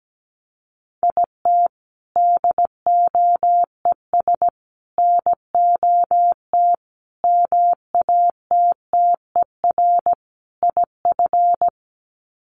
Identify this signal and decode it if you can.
Morse code, heard as a keyed tone, about 17 wpm, 700 Hz. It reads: IT DOES NOT MATTER IF
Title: Sets of 5 Words Encore 17wpm